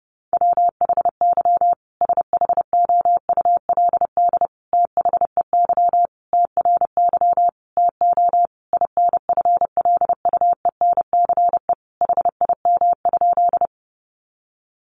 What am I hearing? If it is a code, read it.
Morse code, heard as a keyed tone, about 30 wpm, 700 Hz. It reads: WHY SHOULD THEY TRY TO INFLUENCE HIM?
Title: Sentences from Top 1100 Words 30wpm